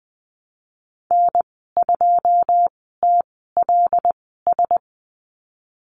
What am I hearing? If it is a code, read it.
Morse code, heard as a keyed tone, about 19 wpm, spaced as for 14 wpm, 700 Hz. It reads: N2TLS
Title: 200 Lesson 32g - Letter k - Call Signs 20wpm